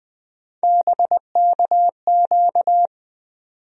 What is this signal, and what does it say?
Morse code, heard as a keyed tone, about 20 wpm, 700 Hz. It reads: BKQ
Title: USA Call Sign Postfixes - 3 Letters 20wpm